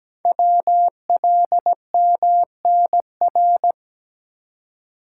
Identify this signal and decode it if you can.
Morse code, heard as a keyed tone, about 17 wpm, 700 Hz. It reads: WLMNR
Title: Code Groups - 5 Letters - 17wpm